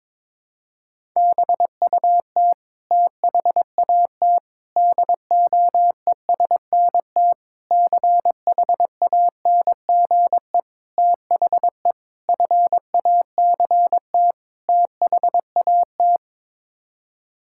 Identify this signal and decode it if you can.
Morse code, heard as a keyed tone, about 22 wpm, 700 Hz. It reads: BUT THAT DOESNT CHANGE THE FACT THAT
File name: practice / 22wpm / Sets of 7 Words - 22wpm